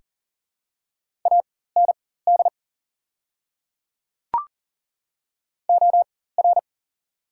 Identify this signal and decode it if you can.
Morse code, heard as a keyed tone, about 40 wpm, spaced as for 19 wpm, 690 Hz. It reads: AND OR
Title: 4x Character Spacing - Top 100 Words - 40wpm